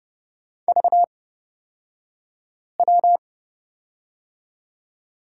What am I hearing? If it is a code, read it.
Morse code, heard as a keyed tone, about 31 wpm, spaced as for 5 wpm, 700 Hz. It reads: VW
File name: Lesson 38d - Letter q - Two Characters 30wpm